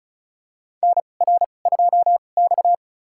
Morse code, heard as a keyed tone, about 36 wpm, spaced as for 26 wpm, 690 Hz. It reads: NR2X
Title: USA Call Signs - 4 Characters - 2x Character Spacing - 35wpm